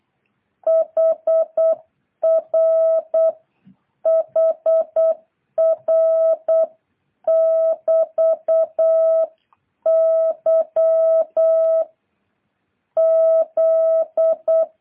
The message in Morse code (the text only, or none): HR HR <BT> Y Z